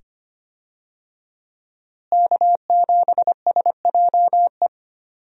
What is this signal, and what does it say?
Morse code, heard as a keyed tone, about 25 wpm, 700 Hz. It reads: K7SJE
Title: USA Call Signs - 5 Characters - Repeated 2x - 25wpm